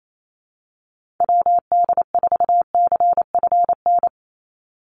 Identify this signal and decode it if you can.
Morse code, heard as a keyed tone, about 28 wpm, 700 Hz. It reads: WD4CFN
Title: NA SOTA Chaser Callsigns 28wpm